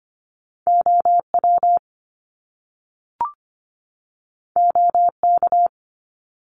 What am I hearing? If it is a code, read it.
Morse code, heard as a keyed tone, about 25 wpm, 700 Hz. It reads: OW OK
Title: Top Two Letter Combinations in Words 25wpm